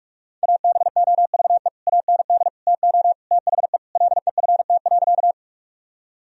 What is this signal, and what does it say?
Morse code, heard as a keyed tone, about 43 wpm, 700 Hz. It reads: ABOVE AND TO THE LEFT.
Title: Sentences from Top 1000 Words - Review Part 2 - 45wpm